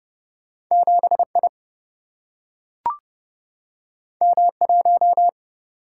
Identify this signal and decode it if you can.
Morse code, heard as a keyed tone, about 30 wpm, 700 Hz. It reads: 7I M1
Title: Single-Letter Single-Number 30wpm